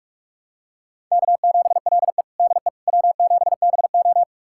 Morse code, heard as a keyed tone, about 46 wpm, 700 Hz. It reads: K7LE DE W7BO
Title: USA Callsign DE Callsign - 4 Characters - 45wpm